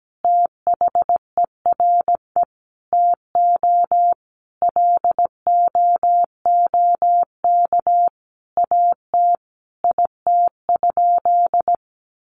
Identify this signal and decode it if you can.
Morse code, heard as a keyed tone, about 17 wpm, 700 Hz. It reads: THERE TO LOOK AT IT?
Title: Sentences from Top 100 Words 17wpm